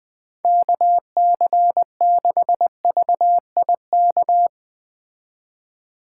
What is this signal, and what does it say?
Morse code, heard as a keyed tone, about 19 wpm, 700 Hz. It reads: KC6VIK
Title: Speed Racing - USA 2x3 Call Signs 17wpm